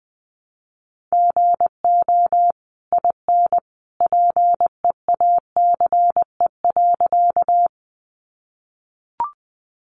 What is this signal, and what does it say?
Morse code, heard as a keyed tone, about 19 wpm, 700 Hz. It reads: GO IN PEACE.
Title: Sentences from Top 1500 Words 20wpm